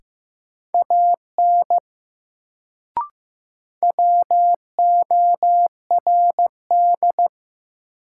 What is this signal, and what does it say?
Morse code, heard as a keyed tone, about 15 wpm, 700 Hz. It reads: AN WORD
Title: Sentences from Top 300 Words - New Word Drill - 15wpm